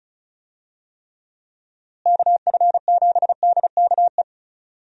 Morse code, heard as a keyed tone, about 36 wpm, 700 Hz. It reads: KF7DKE
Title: USA Call Signs - 6 Characters - Repeated 3x - 35wpm